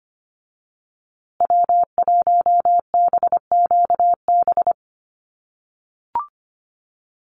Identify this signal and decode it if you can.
Morse code, heard as a keyed tone, about 25 wpm, 700 Hz. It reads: W1BQB
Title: USA Call Signs Repeated 2x 25wpm